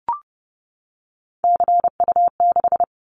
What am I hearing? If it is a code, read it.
Morse code, heard as a keyed tone, about 30 wpm, 700 Hz. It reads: CU6
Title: Callsign Prefixes for Top 100 DXCC - 30wpm